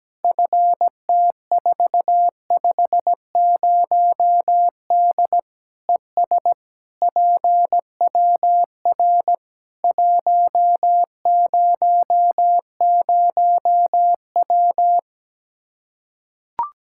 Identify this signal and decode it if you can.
Morse code, heard as a keyed tone, about 17 wpm, 700 Hz. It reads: FT450D ES PWR 100W
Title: K1USN SST QSO Practice 17wpm